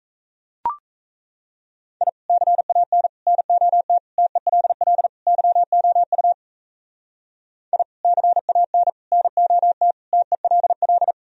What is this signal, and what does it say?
Morse code, heard as a keyed tone, about 36 wpm, 700 Hz. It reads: I CAN NOT TELL YOU I CAN NOT TELL